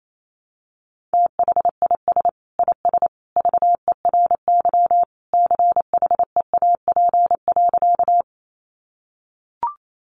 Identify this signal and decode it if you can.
Morse code, heard as a keyed tone, about 28 wpm, 700 Hz. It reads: THIS IS VERY CHEAP.
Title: Sentences from Top 1200 Words 28wpm